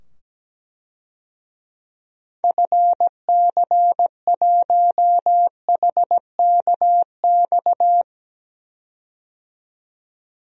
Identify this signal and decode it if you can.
Morse code, heard as a keyed tone, about 17 wpm, 700 Hz. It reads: FC1HKX
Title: DXCC-100 Call Signs - 6 Characters - 17wpm